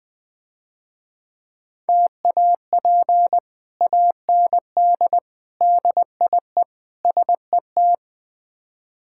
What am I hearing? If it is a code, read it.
Morse code, heard as a keyed tone, about 22 wpm, 700 Hz. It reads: TAP AND DIE SET
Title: Difficult Words - Speed-Racing with Hints - 20wpm